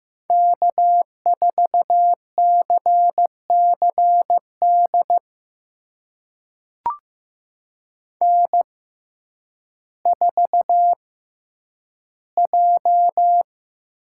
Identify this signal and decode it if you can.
Morse code, heard as a keyed tone, about 15 wpm, spaced as for 9 wpm, 700 Hz. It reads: K4CCD N4J
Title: USA Call Signs - 5 Characters - 6x Character Spacing - 15wpm